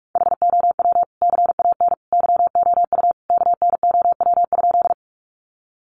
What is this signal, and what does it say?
Morse code, heard as a keyed tone, about 44 wpm, 700 Hz. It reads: HOW CAN YOU KNOW?
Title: Sentences from Top 100 Words 45wpm